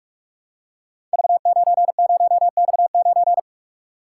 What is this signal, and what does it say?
Morse code, heard as a keyed tone, about 44 wpm, 700 Hz. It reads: V90X9
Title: Code Groups - 5 Letter:Numbers - 45wpm